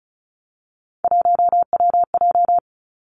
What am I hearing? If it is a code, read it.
Morse code, heard as a keyed tone, about 36 wpm, 700 Hz. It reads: 1WJ
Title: Three Challenging Letter-Number Combinations 35wpm